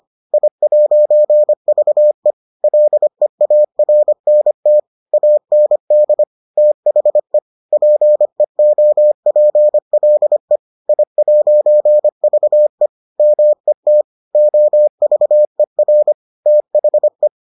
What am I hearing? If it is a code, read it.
Morse code, heard as a keyed tone, about 25 wpm, 600 Hz. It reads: I'VE LEARNT AND THE PEOPLE I'VE MET OVER THE